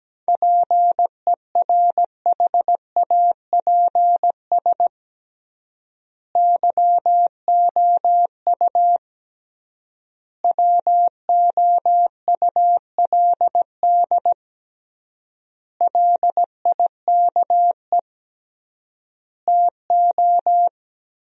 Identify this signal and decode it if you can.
Morse code, heard as a keyed tone, about 17 wpm, 700 Hz. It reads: PERHAPS YOU WOULD LIKE TO
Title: 3x Word-Spacing - Sentences from Top 500 Words - 17wpm